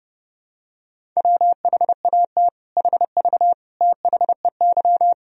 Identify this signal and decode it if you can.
Morse code, heard as a keyed tone, about 32 wpm, 700 Hz. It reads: WHAT HV THEY
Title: Sentences from Top 500 Words and Abbreviations 30wpm